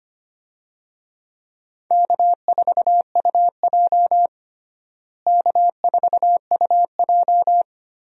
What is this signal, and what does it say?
Morse code, heard as a keyed tone, about 25 wpm, 700 Hz. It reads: K4UJ K4UJ
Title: USA Call Signs - 4 Characters - Repeated 3x - 25wpm